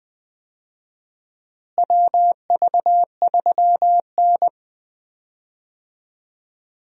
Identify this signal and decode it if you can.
Morse code, heard as a keyed tone, about 20 wpm, 700 Hz. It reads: WV3N
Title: USA Call Signs - 4 Characters - Repeated 3x - 20wpm